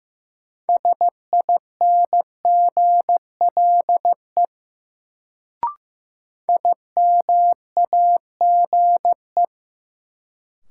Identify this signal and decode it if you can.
Morse code, heard as a keyed tone, about 15 wpm, 700 Hz. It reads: SINGLE IMAGE
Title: Sentences from Top 700 Words - New Word Drill - 15wpm